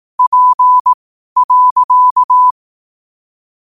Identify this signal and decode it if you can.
Morse code, heard as a keyed tone, about 18 wpm, spaced as for 12 wpm, 1000 Hz. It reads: P.
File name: qotdfiles / 12wpm